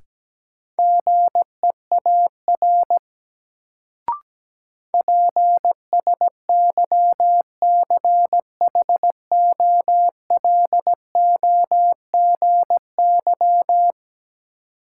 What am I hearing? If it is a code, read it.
Morse code, heard as a keyed tone, about 17 wpm, 700 Hz. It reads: GEAR PSYCHOLOGY
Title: Sentences from Top 1400 Words - New Word Drill - 17wpm